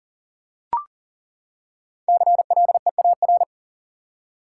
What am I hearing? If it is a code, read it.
Morse code, heard as a keyed tone, about 37 wpm, 700 Hz. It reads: CLEAR
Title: Five Letter Words - 40wpm